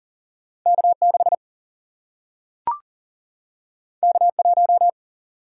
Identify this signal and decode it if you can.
Morse code, heard as a keyed tone, about 40 wpm, 700 Hz. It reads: K6 K1
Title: Single-Letter Single-Number 40wpm